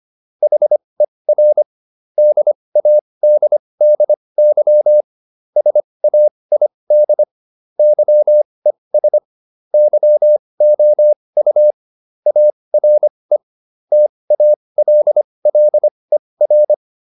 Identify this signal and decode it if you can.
Morse code, heard as a keyed tone, about 25 wpm, spaced as for 20 wpm, 600 Hz. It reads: HER DADDY SAID YES YOU ARE TALLER